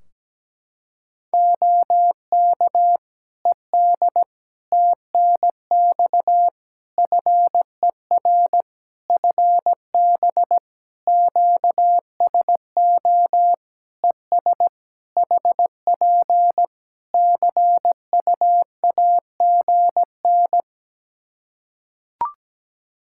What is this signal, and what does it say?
Morse code, heard as a keyed tone, about 17 wpm, 700 Hz. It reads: OK ED TNX FER FB QSO ES HP CUAGN
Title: Practice QSO - Exchange 4 - Ending - 17wpm